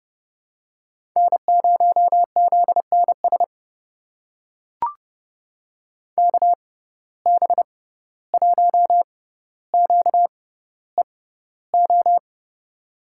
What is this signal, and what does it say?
Morse code, heard as a keyed tone, about 32 wpm, spaced as for 15 wpm, 700 Hz. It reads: N0ZNS KB1QEO
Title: USA Call Signs - 6x Character Spacing - 30wpm